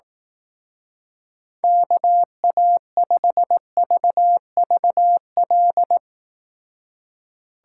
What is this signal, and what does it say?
Morse code, heard as a keyed tone, about 18 wpm, 700 Hz. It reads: KA5VVL
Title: Speed Racing - USA Call Signs 15wpm